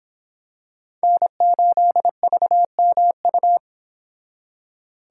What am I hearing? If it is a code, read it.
Morse code, heard as a keyed tone, about 26 wpm, 700 Hz. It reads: N8VMU